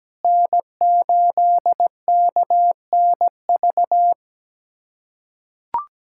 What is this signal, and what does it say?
Morse code, heard as a keyed tone, about 17 wpm, 700 Hz. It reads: N8KNV
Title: USA Callsign DE Callsign - 5 Characters - 17wpm